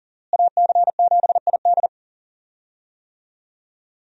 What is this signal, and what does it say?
Morse code, heard as a keyed tone, about 40 wpm, 700 Hz. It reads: AC7ID
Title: USA Call Signs - 5 Characters - Repeated 2x - 40wpm